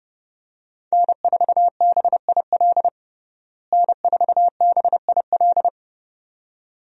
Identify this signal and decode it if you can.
Morse code, heard as a keyed tone, about 30 wpm, 700 Hz. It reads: N4BIL N4BIL
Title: USA Call Signs - 5 Characters - Repeated 2x - 30wpm